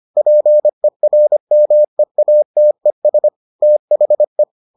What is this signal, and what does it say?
Morse code, heard as a keyed tone, about 25 wpm, 600 Hz. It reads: PERMEATES THE